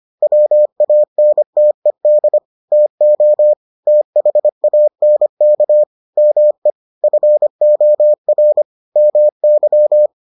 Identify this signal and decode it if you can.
Morse code, heard as a keyed tone, about 25 wpm, 600 Hz. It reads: WANTED TO THANK ME FOR MY